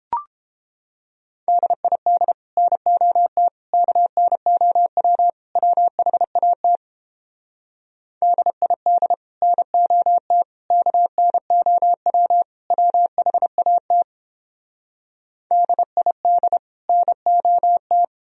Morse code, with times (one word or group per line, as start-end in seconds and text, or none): 1.5-2.3 DID
2.6-3.5 NOT
3.7-5.3 KNOW
5.6-6.8 WHAT
8.2-9.1 DID
9.4-10.4 NOT
10.7-12.4 KNOW
12.7-14.0 WHAT
15.5-16.6 DID
16.9-18.0 NOT